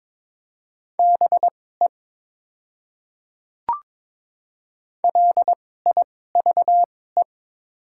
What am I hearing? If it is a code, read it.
Morse code, heard as a keyed tone, about 22 wpm, spaced as for 16 wpm, 700 Hz. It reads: BE LIVE
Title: 2x Character Spacing - Top 200 Words - 22wpm